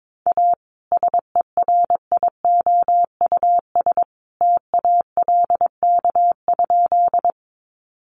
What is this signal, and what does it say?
Morse code, heard as a keyed tone, about 22 wpm, 700 Hz. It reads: A SERIOUS TALK?
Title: Sentences from Top 1100 Words 22wpm